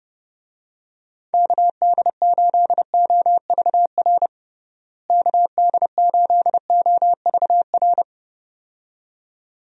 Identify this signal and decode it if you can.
Morse code, heard as a keyed tone, about 29 wpm, 700 Hz. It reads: KD8OVR KD8OVR
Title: USA Call Signs - 6 Characters - Repeated 2x - 30wpm